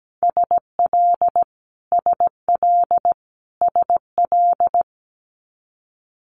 Morse code, heard as a keyed tone, about 17 wpm, 700 Hz. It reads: SL SL SL
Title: Multi-letter Phonemes Repeated 3x 17wpm